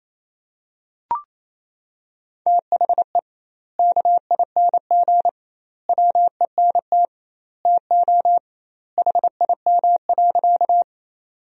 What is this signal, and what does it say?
Morse code, heard as a keyed tone, about 28 wpm, 700 Hz. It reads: THE KING WENT TO HIM.
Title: 2x Word-Spacing - Sentences from Top 400 Words - 28wpm